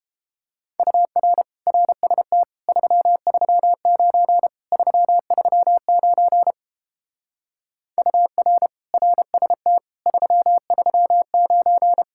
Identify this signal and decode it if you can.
Morse code, heard as a keyed tone, about 31 wpm, 700 Hz. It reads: UR RST 339 339 UR RST 339